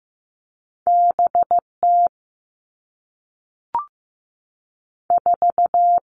Morse code, heard as a keyed tone, about 15 wpm, 700 Hz. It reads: B T 4
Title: Two Challenging Letter-Number Combinations 15wpm